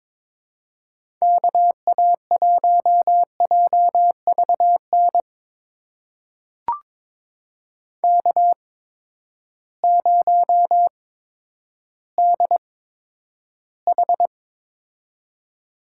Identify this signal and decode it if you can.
Morse code, heard as a keyed tone, about 22 wpm, spaced as for 11 wpm, 700 Hz. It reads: KA1JVN K0DH
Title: USA Call Signs - 8x Character Spacing - 22wpm